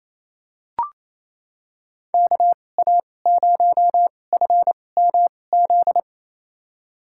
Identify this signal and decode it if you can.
Morse code, heard as a keyed tone, about 28 wpm, spaced as for 20 wpm, 700 Hz. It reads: KA0FMZ
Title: USA Call Signs - 6 Characters - 2x Character Spacing - 28wpm